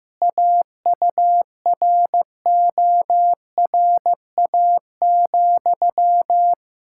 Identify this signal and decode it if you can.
Morse code, heard as a keyed tone, about 15 wpm, 700 Hz. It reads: AURORA,
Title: Practice QSO - Exchange 1 - Beginning - 15wpm